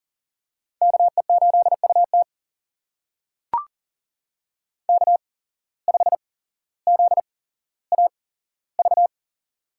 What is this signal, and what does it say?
Morse code, heard as a keyed tone, about 42 wpm, spaced as for 17 wpm, 700 Hz. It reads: KE8UT K5ZAV
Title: USA Call Signs - 5 Characters - 8x Character Spacing - 40wpm